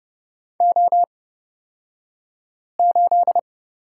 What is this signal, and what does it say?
Morse code, heard as a keyed tone, about 31 wpm, spaced as for 5 wpm, 700 Hz. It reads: O8
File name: Lesson 38d - Letter q - Two Characters 30wpm